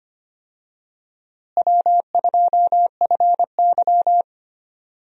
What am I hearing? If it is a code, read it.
Morse code, heard as a keyed tone, about 25 wpm, 700 Hz. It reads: W2FY
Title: USA Call Signs - 4 Characters - Repeated 3x - 25wpm